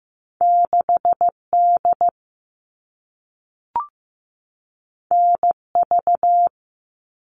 Morse code, heard as a keyed tone, about 15 wpm, 700 Hz. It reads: 6D NV
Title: All Cut Numbers - 2-Digit 15wpm